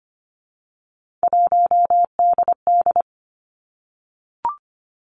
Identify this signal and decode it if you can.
Morse code, heard as a keyed tone, about 25 wpm, 700 Hz. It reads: 1DD
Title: Three Challenging Letter-Number Combinations 25wpm